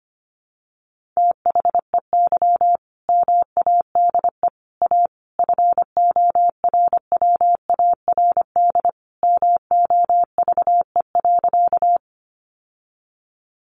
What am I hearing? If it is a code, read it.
Morse code, heard as a keyed tone, about 25 wpm, 700 Hz. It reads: THEY MADE A FORWARD MOVE.
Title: Sentences from Top 1300 Words 25wpm